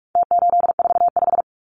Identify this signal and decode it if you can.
Morse code, heard as a keyed tone, about 44 wpm, 700 Hz. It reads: T845
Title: Numbers - Contest Style - 4-Digit 45wpm